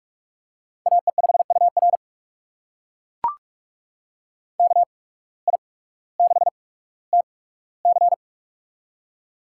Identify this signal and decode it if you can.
Morse code, heard as a keyed tone, about 42 wpm, spaced as for 19 wpm, 700 Hz. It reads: AE5UR KI6TC
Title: USA Call Signs - 5 Characters - 8x Character Spacing - 45wpm